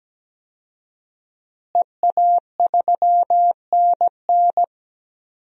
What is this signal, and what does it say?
Morse code, heard as a keyed tone, about 17 wpm, 700 Hz. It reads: EA3NN
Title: DXCC-100 Call Signs - 5 Characters - Repeated 2x - 17wpm